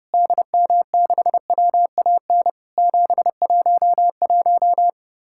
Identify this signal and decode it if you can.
Morse code, heard as a keyed tone, about 32 wpm, 700 Hz. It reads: DM6WAN 711
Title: Worked All Europe DX Contest - QTC Exchange - 30wpm